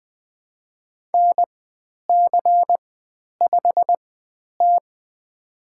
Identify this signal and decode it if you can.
Morse code, heard as a keyed tone, about 20 wpm, spaced as for 10 wpm, 700 Hz. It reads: NC5T
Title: 109 Lesson 18g - Letter m - Call Signs 20wpm